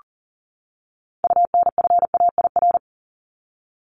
Morse code, heard as a keyed tone, about 40 wpm, 700 Hz. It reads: UNFAIR
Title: Top 1751-2000 Words 40wpm